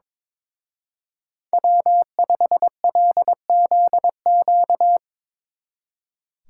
Morse code, heard as a keyed tone, about 25 wpm, 700 Hz. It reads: W5LZQ
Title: Speed Racing - USA 1x3 Call Signs 22wpm